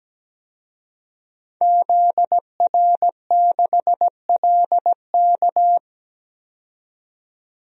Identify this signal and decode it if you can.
Morse code, heard as a keyed tone, about 17 wpm, 700 Hz. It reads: ZR6LK
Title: DXCC-100 Call Signs - 5 Characters - 17wpm